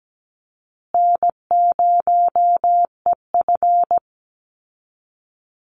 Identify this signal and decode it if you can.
Morse code, heard as a keyed tone, about 17 wpm, 700 Hz. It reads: N0EF
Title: Easy Call Signs Repeated 2x - 17wpm